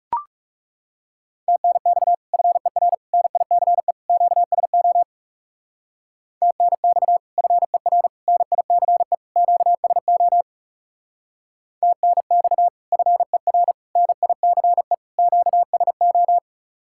Morse code, heard as a keyed tone, about 37 wpm, 700 Hz. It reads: TNX FER NICE QSO TNX FER NICE QSO TNX FER NICE QSO
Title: Speed-Racing Common QSO Phrases - 30wpm